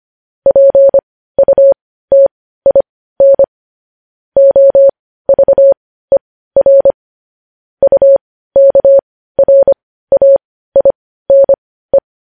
Morse code, heard as a keyed tone, about 25 wpm, spaced as for 15 wpm, 550 Hz. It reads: PUTIN OVER UKRAINE
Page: news-headlines-in-morse-code-at-15-wpm